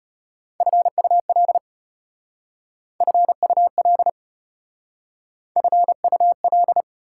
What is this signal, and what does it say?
Morse code, heard as a keyed tone, about 33 wpm, 690 Hz. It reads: FUL FUL FUL